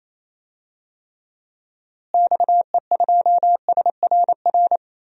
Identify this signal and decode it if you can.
Morse code, heard as a keyed tone, about 28 wpm, 700 Hz. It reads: XE2SRR
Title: DXCC-100 Call Signs - 6 Characters - Repeated 3x - 28wpm